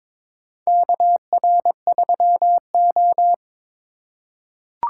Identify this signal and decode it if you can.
Morse code, heard as a keyed tone, about 22 wpm, 700 Hz. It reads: KR3O
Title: USA Call Signs - 4 Characters - 4x Character Spacing - 22wpm